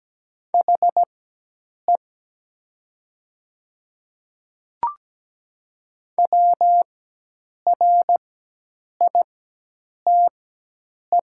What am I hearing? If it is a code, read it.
Morse code, heard as a keyed tone, about 17 wpm, spaced as for 8 wpm, 700 Hz. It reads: HE WRITE
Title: 4x Character Spacing - Top 200 Words - 17wpm